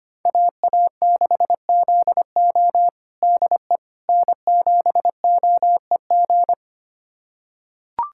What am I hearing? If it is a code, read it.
Morse code, heard as a keyed tone, about 25 wpm, 700 Hz. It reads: AA6ZO DE N7OEG
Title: USA Callsign DE Callsign - 5 Characters - 25wpm